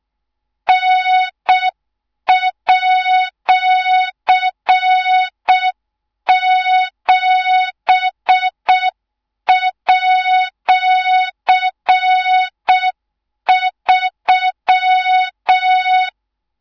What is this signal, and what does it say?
Morse code, heard as a keyed tone, about 6 wpm, 3700 Hz, with noise. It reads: N@7@3
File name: Morze-HA2MN / hang / 24 GYAKORLAT